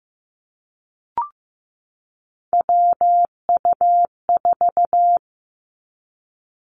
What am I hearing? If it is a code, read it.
Morse code, heard as a keyed tone, about 15 wpm, 700 Hz. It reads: WU4
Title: Three Challenging Letter-Number Combinations 15wpm